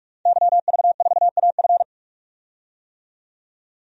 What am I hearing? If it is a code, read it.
Morse code, heard as a keyed tone, about 46 wpm, 700 Hz. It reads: YV4AF
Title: DXCC-100 Call Signs - 5 Characters - 45wpm